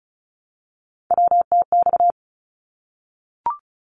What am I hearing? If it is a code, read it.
Morse code, heard as a keyed tone, about 34 wpm, 700 Hz. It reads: WTX
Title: ARRL Sections 35wpm